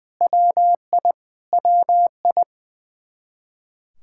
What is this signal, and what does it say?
Morse code, heard as a keyed tone, about 21 wpm, 700 Hz. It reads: WI WI
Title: K1USN SST QSO Practice 20wpm